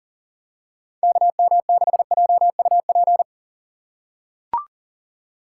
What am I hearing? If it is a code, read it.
Morse code, heard as a keyed tone, about 40 wpm, 700 Hz. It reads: KM6JUP
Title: USA Call Signs - 6 Characters - 6x Character Spacing - 40wpm